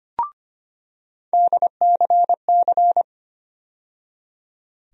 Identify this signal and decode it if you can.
Morse code, heard as a keyed tone, about 25 wpm, 700 Hz. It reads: DCC
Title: USA Call Sign Postfixes - 3 Letters 25wpm